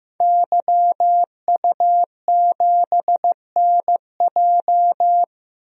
Morse code, heard as a keyed tone, about 15 wpm, 700 Hz. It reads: YU7NJ
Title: DXCC-100 Call Signs - 5 Characters - Repeated 2x - 15wpm